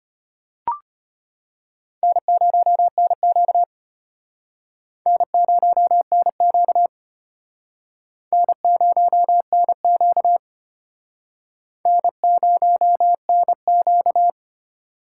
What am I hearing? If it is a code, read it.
Morse code, heard as a keyed tone, about 31 wpm, 700 Hz. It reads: N0NQ N0NQ N0NQ N0NQ
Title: Speed Racing - USA 1x2 Call Signs 25wpm